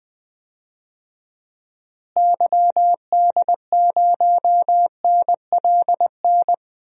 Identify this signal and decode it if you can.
Morse code, heard as a keyed tone, about 20 wpm, 700 Hz. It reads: YD0NLN
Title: DXCC-100 Call Signs - 6 Characters - Repeated 3x - 20wpm